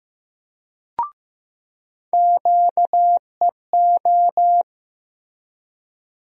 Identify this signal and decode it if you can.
Morse code, heard as a keyed tone, about 15 wpm, 700 Hz. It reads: QEO
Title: USA Call Sign Postfixes - 3 Letters 15wpm